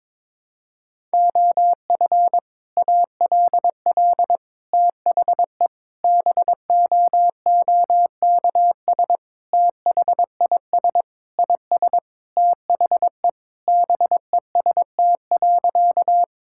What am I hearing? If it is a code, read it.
Morse code, heard as a keyed tone, about 22 wpm, 700 Hz. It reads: OF ALL THE BOOKS THIS IS THE BEST.
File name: practice / 22wpm / Sentences with 3-Word Phrases - 22wpm